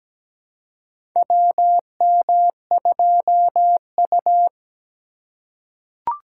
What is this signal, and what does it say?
Morse code, heard as a keyed tone, about 17 wpm, 700 Hz. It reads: WM2U
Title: USA Call Signs - 3x Character Spacing - 17wpm